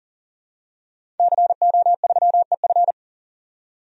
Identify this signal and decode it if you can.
Morse code, heard as a keyed tone, about 40 wpm, 700 Hz. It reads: CO3EF
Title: DXCC-100 Call Signs - 5 Characters - 40wpm